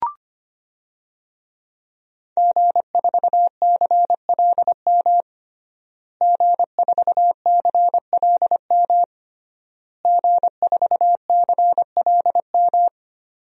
Morse code, heard as a keyed tone, about 25 wpm, 700 Hz. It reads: G4CLM G4CLM G4CLM
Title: DXCC-100 Call Signs - 5 Characters - Repeated 3x - 25wpm